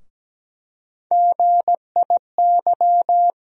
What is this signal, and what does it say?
Morse code, heard as a keyed tone, about 17 wpm, 700 Hz. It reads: GIY
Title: USA Call Sign Postfixes - 3 Letters 17wpm